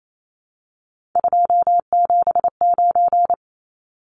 Morse code, heard as a keyed tone, about 28 wpm, 700 Hz. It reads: 279
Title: Numbers - 3-Digit 28wpm